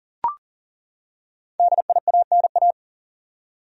Morse code, heard as a keyed tone, about 40 wpm, 700 Hz. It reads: DIANA
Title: USA Common Names - 40wpm